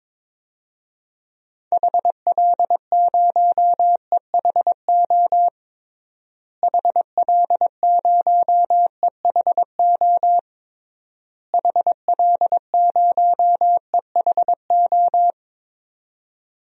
Morse code, heard as a keyed tone, about 22 wpm, 700 Hz. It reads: HL0EHO HL0EHO HL0EHO
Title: DXCC-100 Call Signs - 6 Characters - Repeated 3x - 22wpm